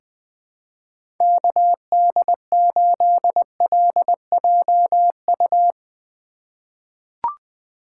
Morse code, heard as a keyed tone, about 20 wpm, 700 Hz. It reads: KD8LJU
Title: USA Call Signs - 6 Characters - 3x Character Spacing - 20wpm